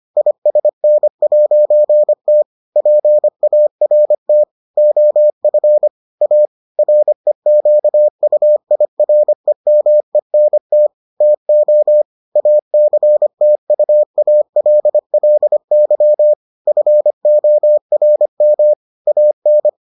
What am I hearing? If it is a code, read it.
Morse code, heard as a keyed tone, about 25 wpm, 600 Hz. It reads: ISN'T PART OF A REQUIREMENT TO ACTUALLY FORM AN